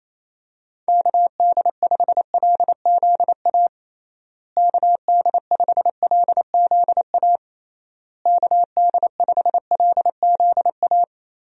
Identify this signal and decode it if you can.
Morse code, heard as a keyed tone, about 28 wpm, 700 Hz. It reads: KD5LZA KD5LZA KD5LZA
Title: USA Call Signs - 6 Characters - Repeated 3x - 28wpm